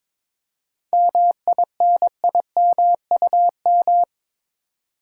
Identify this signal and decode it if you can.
Morse code, heard as a keyed tone, about 22 wpm, 700 Hz. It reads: MINIMUM
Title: Seven Letter Words - 22wpm